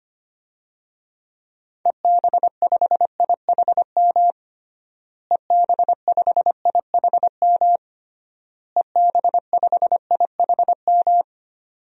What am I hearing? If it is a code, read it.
Morse code, heard as a keyed tone, about 25 wpm, 700 Hz. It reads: EB5IHM EB5IHM EB5IHM
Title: DXCC-100 Call Signs - 6 Characters - Repeated 3x - 25wpm